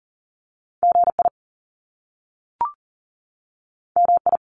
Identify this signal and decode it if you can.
Morse code, heard as a keyed tone, about 40 wpm, 700 Hz. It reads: GI MI
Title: Top Two Letter Combinations in Words 40wpm